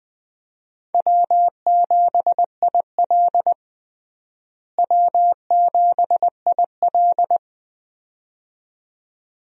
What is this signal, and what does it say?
Morse code, heard as a keyed tone, about 20 wpm, 700 Hz. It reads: W7IL W7IL
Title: USA Call Signs - 4 Characters - Repeated 2x - 20wpm